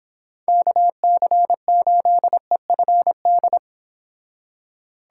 Morse code, heard as a keyed tone, about 26 wpm, 700 Hz. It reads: KC8EFD
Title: Speed Racing - USA Call Signs 22wpm